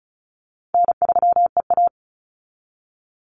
Morse code, heard as a keyed tone, about 34 wpm, 700 Hz. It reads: N3EA
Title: Easy Call Signs Repeated 3x - 35wpm